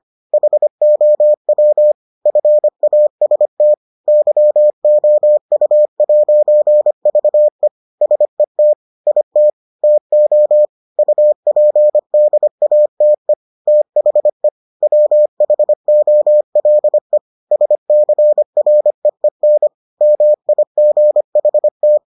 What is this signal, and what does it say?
Morse code, heard as a keyed tone, about 25 wpm, 600 Hz. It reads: HOW FAST YOU'VE SET IT TO UPDATE THE WHOLE SCREEN MIGHT